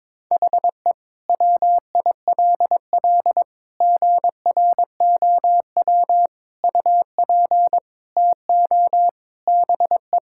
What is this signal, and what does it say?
Morse code, heard as a keyed tone, about 22 wpm, 700 Hz. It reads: HE WILL GROW UP TO BE
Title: Sentences from Top 1000 Words - Review Part 6 - 22wpm